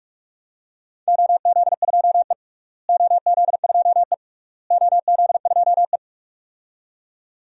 Morse code, heard as a keyed tone, about 45 wpm, 700 Hz. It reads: OZ2E OZ2E OZ2E
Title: DXCC-100 Call Signs - 4 Characters - Repeated 3x - 45wpm